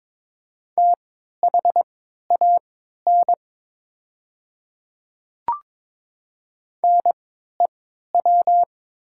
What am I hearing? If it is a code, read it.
Morse code, heard as a keyed tone, about 22 wpm, spaced as for 12 wpm, 700 Hz. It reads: THAN NEW
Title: 3x Character Spacing - Top 300 Words - 22wpm